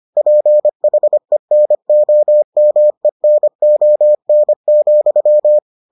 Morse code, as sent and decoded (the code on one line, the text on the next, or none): .--. .... -. --- -- . -. --- -. --..--
PHENOMENON,